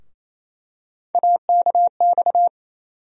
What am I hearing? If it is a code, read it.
Morse code, heard as a keyed tone, about 28 wpm, 700 Hz. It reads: AKX
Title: USA Call Sign Postfixes - 3 Letters 28wpm